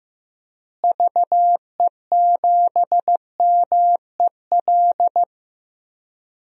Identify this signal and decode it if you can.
Morse code, heard as a keyed tone, about 15 wpm, 700 Hz. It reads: VE7MEL
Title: DXCC-100 Call Signs - 6 Characters - Repeated 3x - 15wpm